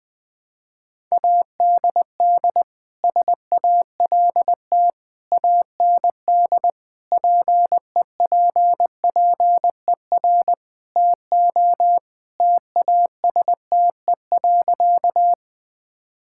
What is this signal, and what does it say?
Morse code, heard as a keyed tone, about 20 wpm, 700 Hz. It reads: ADD SALT AND PEPPER TO TASTE.
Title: Sentences with 5-Word Phrases - Spoken Beginning - 20wpm